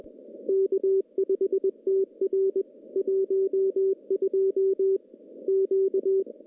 Morse code, heard as a keyed tone, about 21 wpm, 410 Hz. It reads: K5TR 12 Q
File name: contests / CW-SS-k5tr- -0100